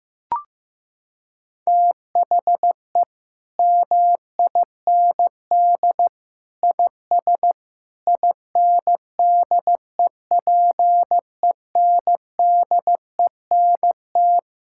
Morse code, heard as a keyed tone, about 16 wpm, 700 Hz. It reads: THE MIND IS INDEPENDENT
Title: Sentences from Top 1000 Words - Review Part 6 - 15wpm